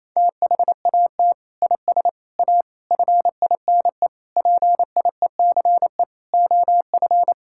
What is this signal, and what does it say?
Morse code, heard as a keyed tone, about 28 wpm, 700 Hz. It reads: THAT IS A FINE PIECE OF
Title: Sentences from Top 1000 Words - Review Part 5 - 28wpm